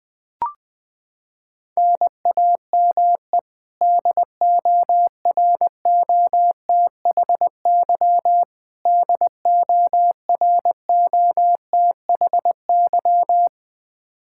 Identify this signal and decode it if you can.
Morse code, heard as a keyed tone, about 20 wpm, 700 Hz. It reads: NAME DOROTHY DOROTHY
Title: Practice QSO - Exchange 1 - Beginning - 20wpm